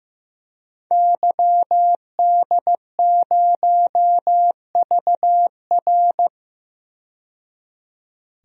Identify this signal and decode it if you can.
Morse code, heard as a keyed tone, about 15 wpm, 700 Hz. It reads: YD0VR
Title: DXCC-100 Call Signs - 5 Characters - 15wpm